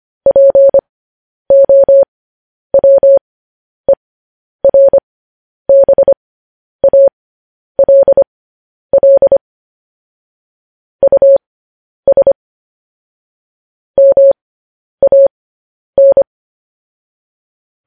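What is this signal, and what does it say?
Morse code, heard as a keyed tone, about 25 wpm, spaced as for 10 wpm, 550 Hz. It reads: POWERBALL US MAN
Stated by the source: Morse code news headlines.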